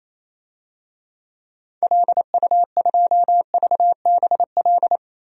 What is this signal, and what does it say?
Morse code, heard as a keyed tone, about 28 wpm, 700 Hz. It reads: LU2VBL
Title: DXCC-100 Call Signs - 6 Characters - 28wpm